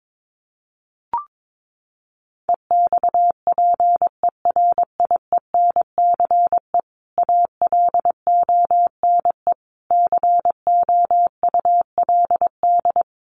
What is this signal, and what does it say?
Morse code, heard as a keyed tone, about 22 wpm, 700 Hz. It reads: EXPERIENCE ALONE COULD
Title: Sentences from Top 1000 Words 22wpm